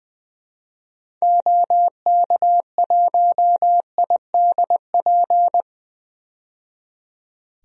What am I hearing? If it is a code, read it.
Morse code, heard as a keyed tone, about 20 wpm, 700 Hz. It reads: OK1IDP
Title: DXCC-100 Call Signs - 6 Characters - 20wpm